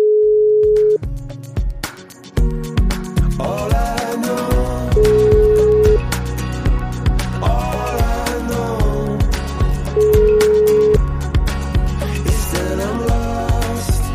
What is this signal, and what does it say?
Ringback tone (Europe, 425 Hz), with noise.